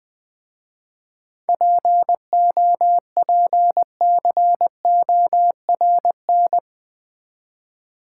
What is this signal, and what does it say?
Morse code, heard as a keyed tone, about 20 wpm, 700 Hz. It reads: POPCORN
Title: Easy Words with Hints - 20wpm